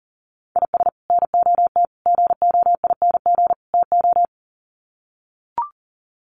Morse code, heard as a keyed tone, about 40 wpm, 700 Hz. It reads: IS NOT GOING TO
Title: Sets of 4 Words 40wpm